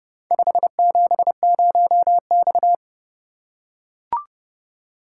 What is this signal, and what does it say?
Morse code, heard as a keyed tone, about 30 wpm, 700 Hz. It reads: 570X